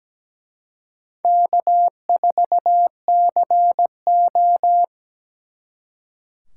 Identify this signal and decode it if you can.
Morse code, heard as a keyed tone, about 20 wpm, 700 Hz. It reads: K4CO
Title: Speed Racing - USA Call Signs 17wpm